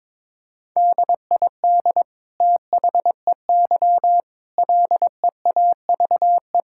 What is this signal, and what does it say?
Morse code, heard as a keyed tone, about 21 wpm, 700 Hz. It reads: DID THEY LEAVE
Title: Sentences from Top 1000 Words - Review Part 5 - 22wpm